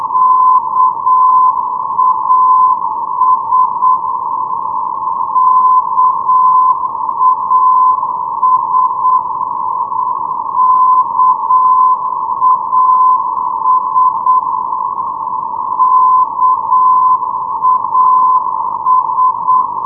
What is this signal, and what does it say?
Morse code, heard as a keyed tone, about 8 wpm, 1020 Hz. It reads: KAS KAS KAS KAS
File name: europe / d-6092-calden-kas-349-k-luebbecke-d-rb-2013